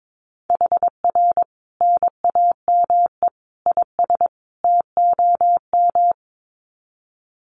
Morse code, heard as a keyed tone, about 22 wpm, 700 Hz. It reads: HR NAME IS TOM
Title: Common QSO Phrases 22wpm